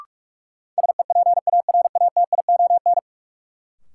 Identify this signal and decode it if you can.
Morse code, heard as a keyed tone, about 46 wpm, 700 Hz. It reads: SEPARATION
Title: Sets of 5 Words - New Word Drill - 45wpm